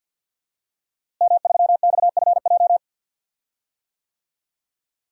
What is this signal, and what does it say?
Morse code, heard as a keyed tone, about 46 wpm, 700 Hz. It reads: M3XFJ
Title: DXCC-100 Call Signs - 5 Characters - Repeated 2x - 50wpm